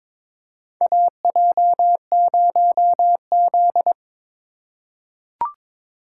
Morse code, heard as a keyed tone, about 23 wpm, 700 Hz. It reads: AJ0Z
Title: USA Call Signs - 4 Characters - 2x Character Spacing - 22wpm